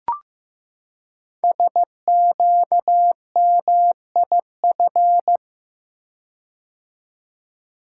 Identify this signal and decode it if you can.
Morse code, heard as a keyed tone, about 15 wpm, 700 Hz. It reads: SQMIF
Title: Code Groups - 5 Letters - 15wpm